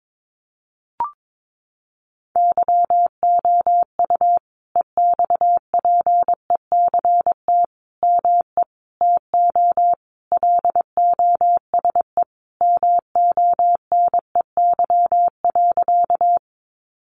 Morse code, heard as a keyed tone, about 22 wpm, 700 Hz. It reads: YOU EXPECT ME TO LOSE MONEY.
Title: Sentences from Top 1100 Words 22wpm